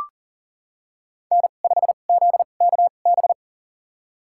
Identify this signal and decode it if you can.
Morse code, heard as a keyed tone, about 38 wpm, spaced as for 28 wpm, 700 Hz. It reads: N5ZKB
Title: USA Call Signs - 2x Character Spacing - 40wpm